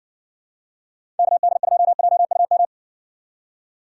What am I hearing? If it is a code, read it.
Morse code, heard as a keyed tone, about 55 wpm, 700 Hz. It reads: KD2JUM